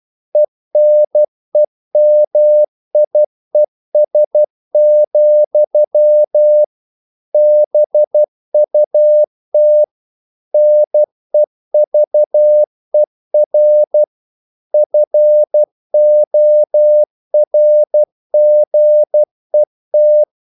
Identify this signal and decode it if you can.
Morse code, heard as a keyed tone, about 12 wpm, 600 Hz. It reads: ENEMIES, BUT NEVER FORGET
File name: CW / Practice / EnglishQuotations0312